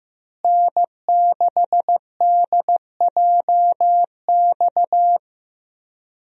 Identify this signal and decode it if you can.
Morse code, heard as a keyed tone, about 15 wpm, 700 Hz. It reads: N6DJX
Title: USA Call Signs - 5 Characters - Repeated 2x - 15wpm